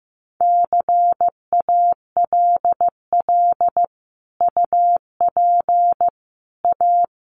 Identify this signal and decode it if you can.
Morse code, heard as a keyed tone, about 15 wpm, 700 Hz. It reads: CALL UP A
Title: Sentences from Top 1500 Words 15wpm